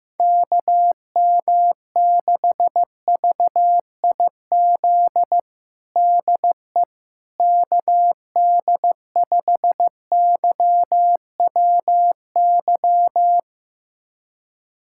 Morse code, heard as a keyed tone, about 16 wpm, 700 Hz. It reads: KM6VIZ DE KD5YWY
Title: USA Callsign DE Callsign - 6 Characters - 15wpm